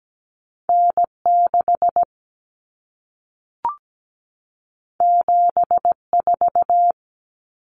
Morse code, heard as a keyed tone, about 17 wpm, 700 Hz. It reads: N6 74
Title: Common Cut Numbers - 2-Digit 17wpm